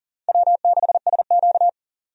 Morse code, heard as a keyed tone, about 38 wpm, 700 Hz. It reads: W6SQ
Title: USA Call Signs - 4 Characters - 40wpm